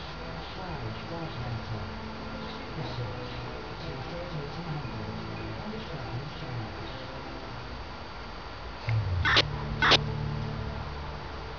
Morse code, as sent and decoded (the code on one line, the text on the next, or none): none